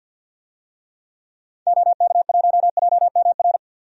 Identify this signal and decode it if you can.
Morse code, heard as a keyed tone, about 50 wpm, 700 Hz. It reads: OK1JMR
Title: DXCC-100 Call Signs - 6 Characters - 50wpm